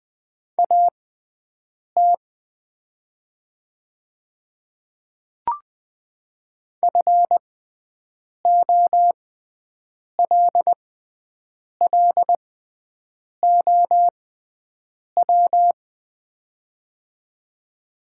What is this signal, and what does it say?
Morse code, heard as a keyed tone, about 20 wpm, spaced as for 7 wpm, 700 Hz. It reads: AT FOLLOW
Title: 6x Character Spacing - Top 200 Words - 20wpm